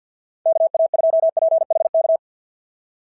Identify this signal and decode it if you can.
Morse code, heard as a keyed tone, about 51 wpm, 690 Hz. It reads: KA2PHK
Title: SOTA QSO Practice 50wpm